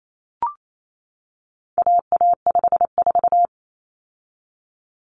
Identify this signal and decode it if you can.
Morse code, heard as a keyed tone, about 28 wpm, 700 Hz. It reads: AA54
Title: Common Cut Numbers - 4-Digit 28wpm